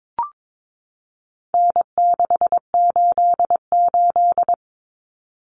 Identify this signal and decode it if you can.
Morse code, heard as a keyed tone, about 22 wpm, 700 Hz. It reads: N688
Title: Common Cut Numbers - 4-Digit 22wpm